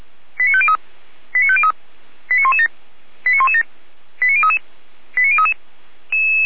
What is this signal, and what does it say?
Five-tone selective call (ZVEI), heard as quick runs of five tones, tones of about 70 ms. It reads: 89503 89503 89117 89117 89033 89033